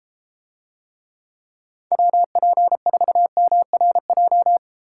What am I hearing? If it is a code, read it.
Morse code, heard as a keyed tone, about 32 wpm, 700 Hz. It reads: WP4MRJ